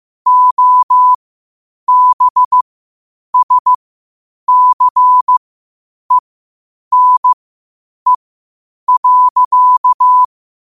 Morse code, heard as a keyed tone, about 15 wpm, spaced as for 8 wpm, 1000 Hz. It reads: OBSCENE.
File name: qotdfiles / 8wpm